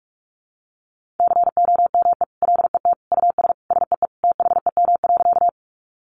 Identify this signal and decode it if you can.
Morse code, heard as a keyed tone, about 44 wpm, 700 Hz. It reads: COME LET US SEE THEM.
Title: Sentences from Top 300 Words 45wpm